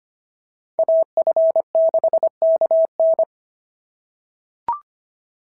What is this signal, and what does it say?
Morse code, heard as a keyed tone, about 25 wpm, 690 Hz. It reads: AF6KN
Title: SOTA QSO Practice 25wpm